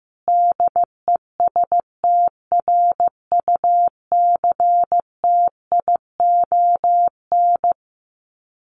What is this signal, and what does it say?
Morse code, heard as a keyed tone, about 14 wpm, 700 Hz. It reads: DESTRUCTION
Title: Sets of 5 Words Encore 15wpm